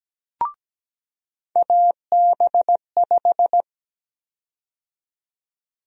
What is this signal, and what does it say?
Morse code, heard as a keyed tone, about 17 wpm, 700 Hz. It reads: AB5
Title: USA Call Sign Prefixes - 17wpm